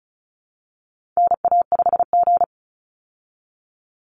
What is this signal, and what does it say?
Morse code, heard as a keyed tone, about 34 wpm, 700 Hz. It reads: NA5G
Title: Call Signs Repeated 3x - 35wpm